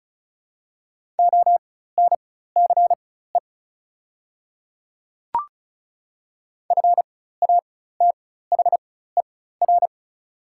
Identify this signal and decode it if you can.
Morse code, heard as a keyed tone, about 36 wpm, spaced as for 16 wpm, 700 Hz. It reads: ONCE FATHER
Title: 4x Character Spacing - Top 300 Words - 35wpm